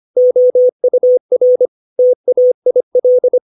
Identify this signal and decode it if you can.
Morse code, heard as a keyed tone, about 25 wpm, 500 Hz. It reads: OUR TAIL
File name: 632 words / 632 Words 25WPM